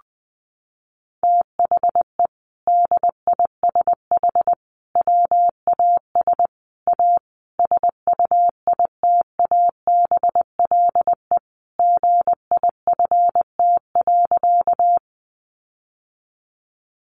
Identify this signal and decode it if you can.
Morse code, heard as a keyed tone, about 20 wpm, 700 Hz. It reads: THE DISH WAS A SUITABLE GIFT.
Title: Sentences from Top 1400 Words 20wpm